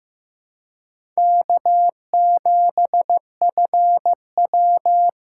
Morse code, heard as a keyed tone, about 15 wpm, 700 Hz. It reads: K7FW
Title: USA Call Signs - 4 Characters - 4x Character Spacing - 15wpm